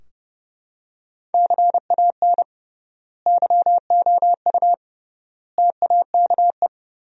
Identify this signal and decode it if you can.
Morse code, heard as a keyed tone, about 30 wpm, 700 Hz. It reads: CAN YOU TAKE
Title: 3x Word-Spacing - Sentences from Top 200 Words - 30wpm